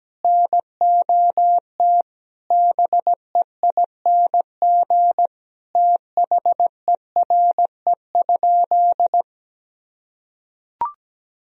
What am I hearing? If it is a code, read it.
Morse code, heard as a keyed tone, about 17 wpm, 700 Hz. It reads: NOT BEING THERE?
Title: Sentences from Top 1000 Words - Review Part 3 - 17wpm